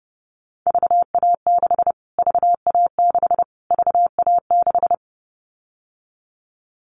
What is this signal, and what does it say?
Morse code, heard as a keyed tone, about 30 wpm, 700 Hz. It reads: VA6 VA6 VA6
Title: Callsign Prefixes for Top 25 DXCC - Repeated 3x - 30wpm